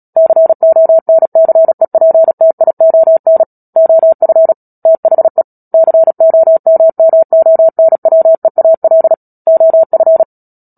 Morse code, heard as a keyed tone, about 37 wpm, 650 Hz. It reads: CONCEPTION OF THE COMMONWEAL OF
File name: Book Two / 10-Chapter 10 - Epilogue - 36 WPM